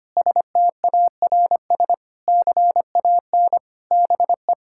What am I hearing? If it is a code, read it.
Morse code, heard as a keyed tone, about 25 wpm, 700 Hz. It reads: STARS CAN BE
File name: practice / 25wpm / Difficult Sentences with Hints - 25wpm